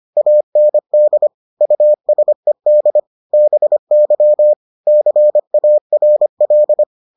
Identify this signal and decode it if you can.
Morse code, heard as a keyed tone, about 25 wpm, 600 Hz. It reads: AND USED BY CARL